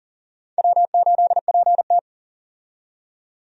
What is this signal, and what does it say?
Morse code, heard as a keyed tone, about 42 wpm, 690 Hz. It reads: W8PT
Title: USA Call Signs - 4 Characters - 40wpm